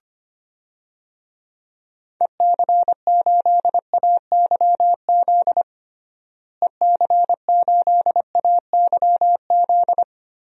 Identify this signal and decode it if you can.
Morse code, heard as a keyed tone, about 25 wpm, 700 Hz. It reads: EC8AYZ EC8AYZ
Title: DXCC-100 Call Signs - 6 Characters - Repeated 3x - 25wpm